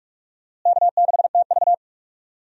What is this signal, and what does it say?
Morse code, heard as a keyed tone, about 46 wpm, 700 Hz. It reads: K6TV
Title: USA Call Signs - 4 Characters - Repeated 3x - 45wpm